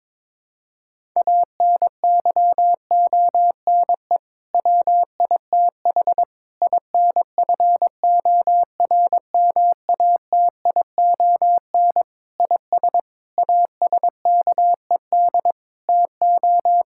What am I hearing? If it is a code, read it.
Morse code, heard as a keyed tone, about 22 wpm, 700 Hz. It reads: ANYONE WITH INFORMATION IS ASKED TO
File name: practice / 22wpm / Sets of 7 Words - 22wpm